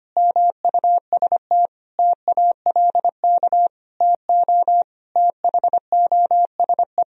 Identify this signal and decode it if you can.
Morse code, heard as a keyed tone, about 25 wpm, 700 Hz. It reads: MUST TALK TO THOSE
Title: Sentences from Top 1000 Words - Review Part 1 - 25wpm